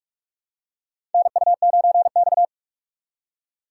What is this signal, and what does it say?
Morse code, heard as a keyed tone, about 43 wpm, 700 Hz. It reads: NU9X
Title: USA Call Signs - 4 Characters - 3x Character Spacing - 45wpm